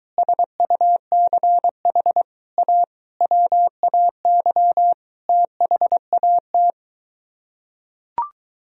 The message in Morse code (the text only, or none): SUCH A WAY THAT